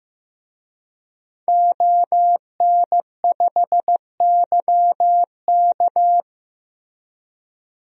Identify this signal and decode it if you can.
Morse code, heard as a keyed tone, about 15 wpm, 700 Hz. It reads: ON5YK
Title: DXCC-100 Call Signs - 5 Characters - Repeated 2x - 15wpm